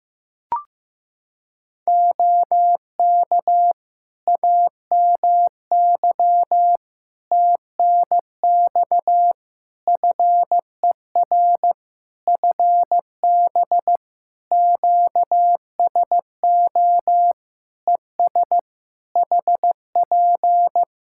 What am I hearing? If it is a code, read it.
Morse code, heard as a keyed tone, about 15 wpm, 700 Hz. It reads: OK AMY TNX FER FB QSO ES HP
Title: Practice QSO - Exchange 4 - Ending - 15wpm